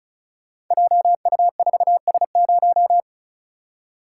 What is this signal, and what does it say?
Morse code, heard as a keyed tone, about 36 wpm, 690 Hz. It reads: JU4S0
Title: Code Groups - 5 Letter:Numbers - 35wpm